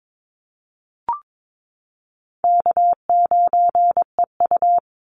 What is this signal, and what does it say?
Morse code, heard as a keyed tone, about 22 wpm, 700 Hz. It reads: K9EU
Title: Call Signs Repeated 3x - 22wpm